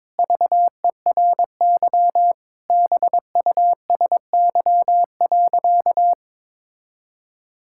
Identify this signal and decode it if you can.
Morse code, heard as a keyed tone, about 22 wpm, 700 Hz. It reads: VERY BUSY.
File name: Sentences with 7-Word Phrases - 22wpm